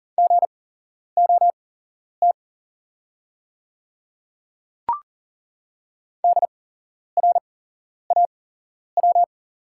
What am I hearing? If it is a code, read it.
Morse code, heard as a keyed tone, about 40 wpm, spaced as for 11 wpm, 700 Hz. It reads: GOT DRAW
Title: 8x Character Spacing - Top 300 Words - 40wpm